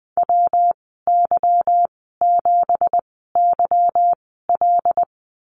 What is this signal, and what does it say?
Morse code, heard as a keyed tone, about 20 wpm, spaced as for 15 wpm, 700 Hz. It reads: WY7YL
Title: 173 Lesson 28g - Number 9 - Call Signs 20wpm